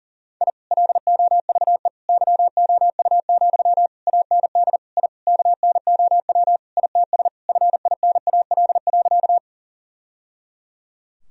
Morse code, heard as a keyed tone, about 40 wpm, 700 Hz. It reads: I LOVE YOU, AND I KNOW ITS FINAL.
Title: Sentences from Top 1000 Words - Review Part 2 - 40wpm